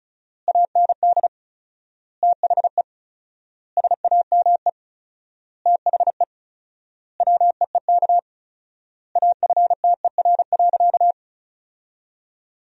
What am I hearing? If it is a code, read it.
Morse code, heard as a keyed tone, about 36 wpm, 700 Hz. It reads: AND THE SAME THE WEEK AFTER.
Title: 4x Word-Spacing - Sentences from Top 500 Words - 35wpm